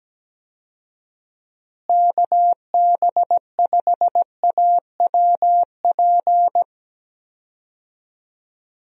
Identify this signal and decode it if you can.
Morse code, heard as a keyed tone, about 17 wpm, 700 Hz. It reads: KB5AWP
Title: USA Call Signs - 6 Characters - Repeated 2x - 17wpm